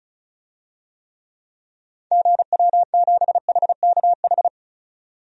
Morse code, heard as a keyed tone, about 36 wpm, 700 Hz. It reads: GW7HKH
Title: DXCC-100 Call Signs - 6 Characters - 35wpm